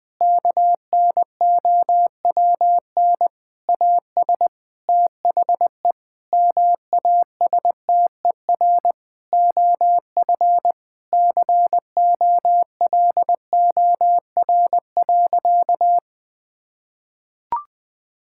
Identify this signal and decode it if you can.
Morse code, heard as a keyed tone, about 19 wpm, 700 Hz. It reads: KNOWN AS THE MASTER OF COLOR.
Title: Sentences with 6-Word Phrases - Spoken Beginning - 20wpm